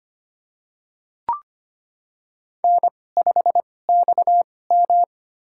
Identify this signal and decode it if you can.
Morse code, heard as a keyed tone, about 25 wpm, spaced as for 18 wpm, 700 Hz. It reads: N5XM
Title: USA Call Signs - 4 Characters - 2x Character Spacing - 25wpm